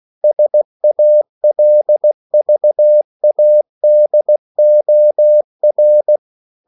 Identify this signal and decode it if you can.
Morse code, heard as a keyed tone, about 16 wpm, 600 Hz. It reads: SALVADOR
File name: CW / Practice / EnglishQuotations0116